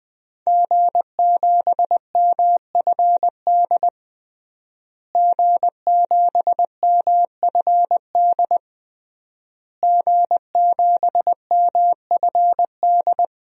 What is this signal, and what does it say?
Morse code, heard as a keyed tone, about 20 wpm, 700 Hz. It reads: G7MFD G7MFD G7MFD
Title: DXCC-100 Call Signs - 5 Characters - Repeated 3x - 20wpm